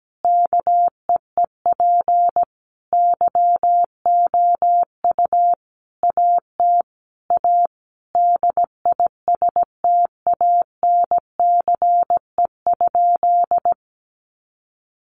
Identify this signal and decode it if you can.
Morse code, heard as a keyed tone, about 17 wpm, 700 Hz. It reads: KEEP YOU AT A DISTANCE?
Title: Sentences from Top 1200 Words 17wpm